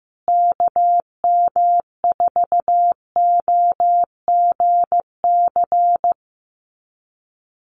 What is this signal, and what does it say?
Morse code, heard as a keyed tone, about 15 wpm, 700 Hz. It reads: KM4OGC
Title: USA Call Signs Repeated 3x 15wpm